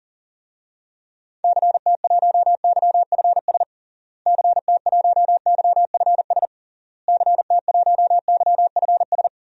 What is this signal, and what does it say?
Morse code, heard as a keyed tone, about 40 wpm, 700 Hz. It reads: CT1YFS CT1YFS CT1YFS
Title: DXCC-100 Call Signs - 6 Characters - Repeated 3x - 40wpm